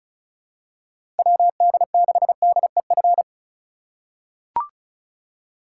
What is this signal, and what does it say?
Morse code, heard as a keyed tone, about 36 wpm, 700 Hz. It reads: WD6DEF
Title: USA Call Signs - 6 Characters - 3x Character Spacing - 35wpm